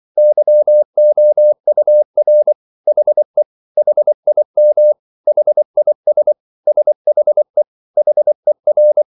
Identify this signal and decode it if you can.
Morse code, heard as a keyed tone, about 24 wpm, 600 Hz. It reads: YOUR HE HIM HIS SHE HER
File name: CW / Practice / 100MostCommonEnglishWords24